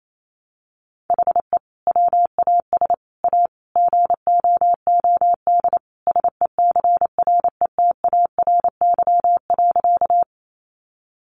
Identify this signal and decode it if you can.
Morse code, heard as a keyed tone, about 28 wpm, 700 Hz. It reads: HE WAS A GOOD SECRETARY.
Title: Sentences from Top 1300 Words 28wpm